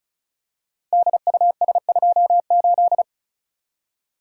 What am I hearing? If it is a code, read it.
Morse code, heard as a keyed tone, about 36 wpm, 700 Hz. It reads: DUS28
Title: Code Groups - 5 Letter:Numbers - 35wpm